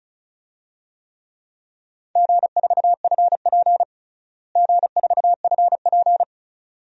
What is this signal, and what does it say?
Morse code, heard as a keyed tone, about 34 wpm, 700 Hz. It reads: G4FP G4FP
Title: DXCC-100 Call Signs - 4 Characters - Repeated 2x - 35wpm